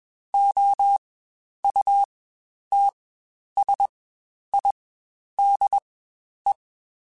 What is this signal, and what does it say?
Morse code, heard as a keyed tone, about 21 wpm, spaced as for 10 wpm, 800 Hz. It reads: OUTSIDE
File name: Spaced Characters / Spaced Characters-mp3s / Weather 4 24 sp @20